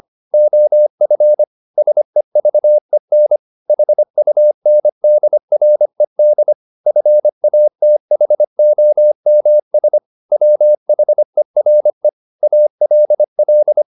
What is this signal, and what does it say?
Morse code, heard as a keyed tone, about 25 wpm, 600 Hz. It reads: OF SEVEN HUNDRED FATHOMS WHERE ALL
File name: the old man and the sea part 1 25wpm